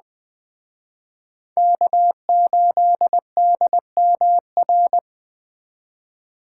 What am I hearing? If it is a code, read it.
Morse code, heard as a keyed tone, about 20 wpm, 700 Hz. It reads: K8DMR
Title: Speed Racing - USA Call Signs 17wpm